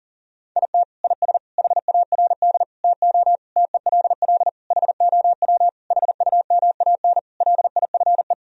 Morse code, heard as a keyed tone, about 40 wpm, 700 Hz. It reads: IT IS HARD TO TELL HOW HUMAN LIFE
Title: Sentences from Top 1000 Words - Review Part 3 - 40wpm